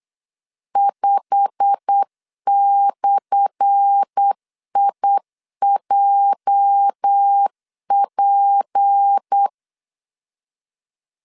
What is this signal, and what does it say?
Morse code, heard as a keyed tone, about 8 wpm, 800 Hz. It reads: 5/IJP
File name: LEZIONI / 10 SIGNS / 10A SLASH 8wpm 3452